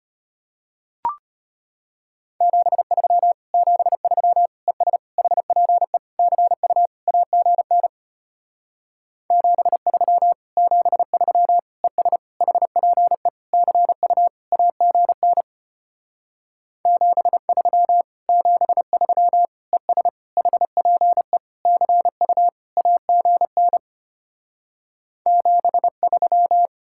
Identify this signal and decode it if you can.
Morse code, heard as a keyed tone, about 33 wpm, 700 Hz. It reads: 73 73 ES HPE CU AGN 73 73 ES HPE CU AGN 73 73 ES HPE CU AGN 73